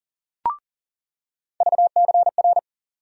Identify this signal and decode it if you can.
Morse code, heard as a keyed tone, about 40 wpm, 700 Hz. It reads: VCR
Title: USA Call Sign Postfixes - 3 Letters 40wpm